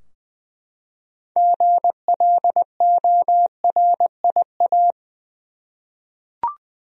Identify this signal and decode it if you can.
Morse code, heard as a keyed tone, about 20 wpm, 700 Hz. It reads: GLORIA
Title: USA Common Names - 20wpm